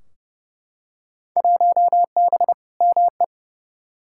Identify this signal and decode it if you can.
Morse code, heard as a keyed tone, about 32 wpm, 700 Hz. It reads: 1B ME
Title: Field Day Exchanges - 30wpm